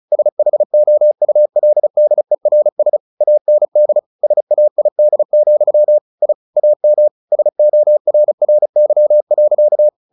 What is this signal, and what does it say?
Morse code, heard as a keyed tone, about 34 wpm, 600 Hz. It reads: SHOULDERS AND SAID, I AM SORRY.
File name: the old man and the sea / the old man and the sea 1 35wpm